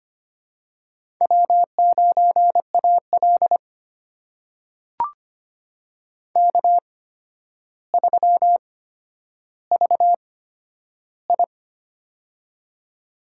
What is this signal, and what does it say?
Morse code, heard as a keyed tone, about 25 wpm, spaced as for 11 wpm, 700 Hz. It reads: W9AL K3VI
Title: USA Call Signs - 4 Characters - 8x Character Spacing - 25wpm